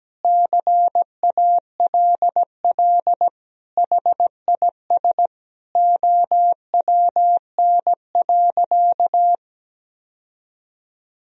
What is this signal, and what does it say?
Morse code, heard as a keyed tone, about 17 wpm, 700 Hz. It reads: CALL HIS OWN.
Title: Sentences from Top 1000 Words - Review Part 2 - 17wpm